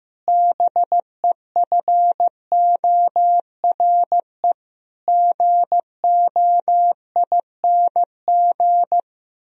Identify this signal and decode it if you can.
Morse code, heard as a keyed tone, about 15 wpm, 700 Hz. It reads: BEFORE GOING
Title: Sentences from Top 1000 Words - Review Part 3 - 15wpm